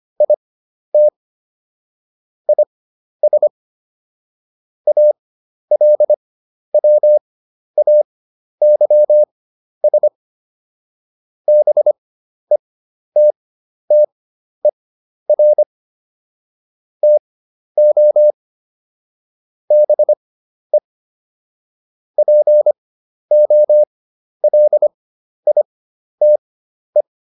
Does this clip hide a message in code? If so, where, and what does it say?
Yes: Morse code from 0.2 to 27.0 s, heard as a keyed tone, about 25 wpm, spaced as for 11 wpm, 600 Hz. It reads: IT IS ALWAYS BETTER TO BE POLITE